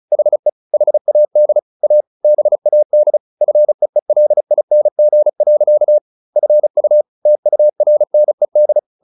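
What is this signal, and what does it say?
Morse code, heard as a keyed tone, about 34 wpm, 600 Hz. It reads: HE HAD A BAD FEELING. FU TURNED
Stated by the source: Morse practice at 35 wpm from CW med Gnister, sending Ken Follett's novel Never in English.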